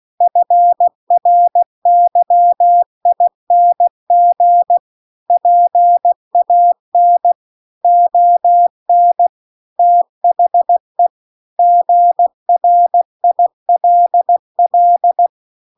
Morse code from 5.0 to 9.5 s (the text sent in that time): PAN ON